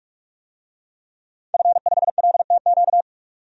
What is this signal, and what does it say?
Morse code, heard as a keyed tone, about 42 wpm, 700 Hz. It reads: F5LTQ
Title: DXCC-100 Call Signs - 5 Characters - 45wpm